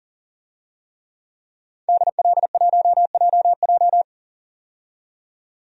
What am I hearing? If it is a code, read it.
Morse code, heard as a keyed tone, about 40 wpm, 700 Hz. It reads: DL1JJ
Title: DXCC-100 Call Signs - 5 Characters - 40wpm